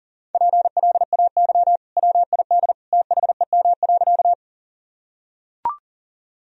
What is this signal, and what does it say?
Morse code, heard as a keyed tone, about 40 wpm, 700 Hz. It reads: PLAY WID THEM.
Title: Sentences from Top 200 Words and Abbreviations 40wpm